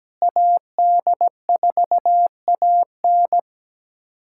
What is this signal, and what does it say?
Morse code, heard as a keyed tone, about 17 wpm, 700 Hz. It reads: AD4AN
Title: USA Callsign DE Callsign - 5 Characters - 17wpm